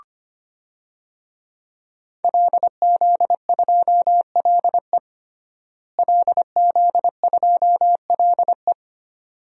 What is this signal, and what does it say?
Morse code, heard as a keyed tone, about 25 wpm, 700 Hz. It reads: LZ2LE LZ2LE
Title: DXCC-100 Call Signs - 5 Characters - Repeated 2x - 25wpm